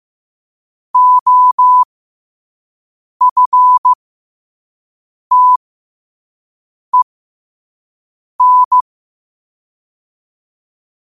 Morse code, heard as a keyed tone, about 15 wpm, spaced as for 5 wpm, 1000 Hz. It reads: OFTEN
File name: qotdfiles / 5wpm